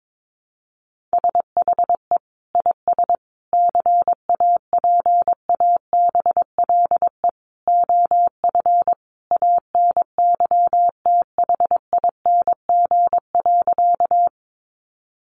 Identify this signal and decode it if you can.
Morse code, heard as a keyed tone, about 22 wpm, 700 Hz. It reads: SHE IS CAPABLE OF ANYTHING.
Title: Sentences from Top 1200 Words 22wpm